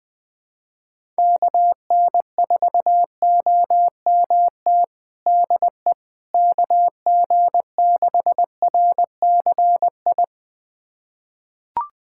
Morse code, heard as a keyed tone, about 20 wpm, 700 Hz. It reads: KN4OMT DE KG6RCI
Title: USA Callsign DE Callsign - 6 Characters - 20wpm